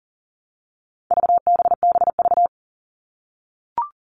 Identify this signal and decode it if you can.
Morse code, heard as a keyed tone, about 40 wpm, 700 Hz. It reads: VBBV
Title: All Cut Numbers - 4-Digit 40wpm